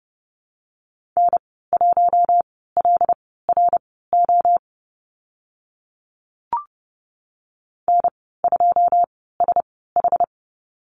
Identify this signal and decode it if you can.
Morse code, heard as a keyed tone, about 30 wpm, spaced as for 17 wpm, 700 Hz. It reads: N1LRO N2SH
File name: Lesson 14g - Number 2 - Call Signs 30wpm